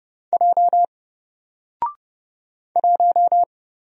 Morse code, heard as a keyed tone, about 30 wpm, 700 Hz. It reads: J 1
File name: Single-Letter-Number-30wpm